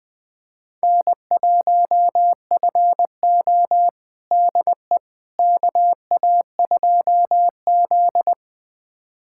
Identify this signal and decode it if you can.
Morse code, heard as a keyed tone, about 20 wpm, 700 Hz. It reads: N1FO DE KA2Z
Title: USA Callsign DE Callsign - 4 Characters - 20wpm